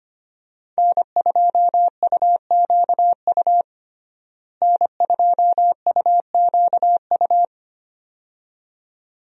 Morse code, heard as a keyed tone, about 25 wpm, 700 Hz. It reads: N2UQU N2UQU
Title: USA Call Signs - 5 Characters - Repeated 2x - 25wpm